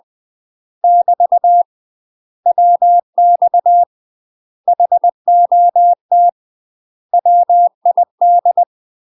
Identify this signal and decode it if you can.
Morse code, heard as a keyed tone, about 20 wpm, 700 Hz. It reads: <BT> WX HOT WID
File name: UK MASTER QSOs TRAINING / 20WPM / UK MASTER QSOs 0066